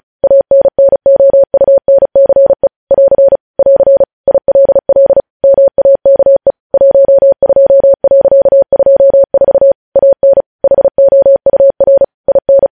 Morse code, heard as a keyed tone, about 36 wpm, 550 Hz. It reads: ANNOUNCE <AR> <AR> ILL MAKE 12.24 AN HOUR IN
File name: podcast / News headlines-1x-35x35